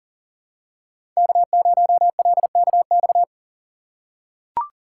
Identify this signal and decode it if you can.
Morse code, heard as a keyed tone, about 37 wpm, 700 Hz. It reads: K0LKX
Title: USA Call Signs - 5 Characters - 3x Character Spacing - 40wpm